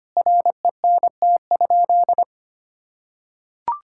Morse code, heard as a keyed tone, about 25 wpm, 700 Hz. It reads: RENT?
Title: Sentences from Top 1000 Words - Review Part 5 - 25wpm